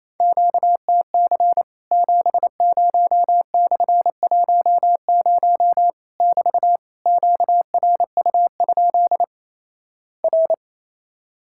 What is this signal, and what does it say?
Morse code, heard as a keyed tone, about 28 wpm, 700 Hz. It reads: QTC 70/10 <BT> QRU? R
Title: Worked All Europe DX Contest - QTC Exchange - 28wpm